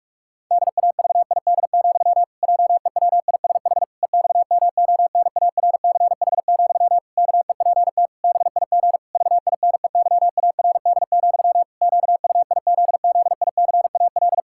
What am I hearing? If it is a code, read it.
Morse code, heard as a keyed tone, about 46 wpm, 700 Hz. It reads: DAVID, JEWISH EXMONARCH, KEPT BIG VINEYARD, QUIZZICAL
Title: Pangram Challenge v2 - 45wpm